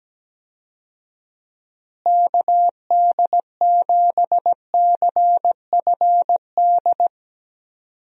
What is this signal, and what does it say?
Morse code, heard as a keyed tone, about 17 wpm, 700 Hz. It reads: KD7CFD
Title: USA Call Signs - 6 Characters - Repeated 3x - 17wpm